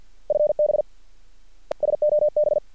Morse code, heard as a keyed tone, about 50 wpm, 600 Hz. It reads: FB SOB
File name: archived / CRSnet